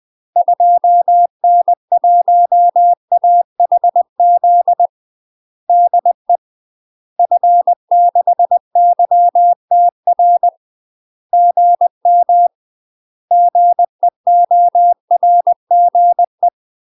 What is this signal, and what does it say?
Morse code, heard as a keyed tone, about 20 wpm, 700 Hz. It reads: 2N1AHZ DE F6YTR GM GEORGE
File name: UK MASTER QSOs TRAINING/20WPM